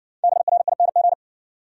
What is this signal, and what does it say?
Morse code, heard as a keyed tone, about 58 wpm, 700 Hz. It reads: BRING